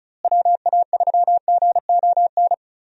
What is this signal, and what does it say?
Morse code, heard as a keyed tone, about 34 wpm, 700 Hz. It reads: WA3GON
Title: USA Call Signs - 6 Characters - Repeated 2x - 35wpm